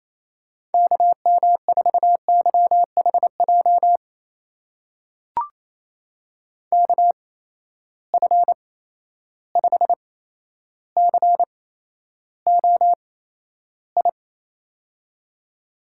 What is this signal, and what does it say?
Morse code, heard as a keyed tone, about 28 wpm, spaced as for 12 wpm, 700 Hz. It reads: KM4YHJ KF5COI
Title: USA Call Signs - 6 Characters - 8x Character Spacing - 28wpm